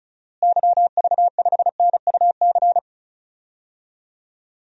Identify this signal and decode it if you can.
Morse code, heard as a keyed tone, about 34 wpm, 700 Hz. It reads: YV5NUC
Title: DXCC-100 Call Signs - 6 Characters - Repeated 2x - 35wpm